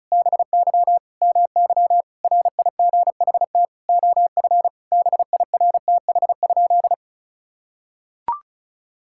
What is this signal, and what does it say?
Morse code, heard as a keyed tone, about 36 wpm, 700 Hz. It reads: BY MY RIGHT OF BIRTH?
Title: Sentences from Top 1000 Words - Review Part 6 - 35wpm